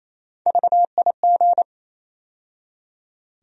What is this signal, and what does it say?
Morse code, heard as a keyed tone, about 28 wpm, 700 Hz. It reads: VIG
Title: USA Call Sign Postfixes - 3 Letters 28wpm